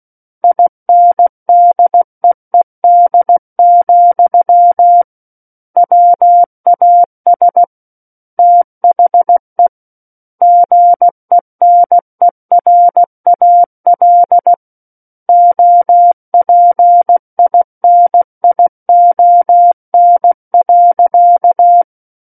Morse code, heard as a keyed tone, about 16 wpm, 700 Hz. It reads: INDEED, WAS THE GENERAL OPINION.
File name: Book One / 07-Chapter 7 - 16 WPM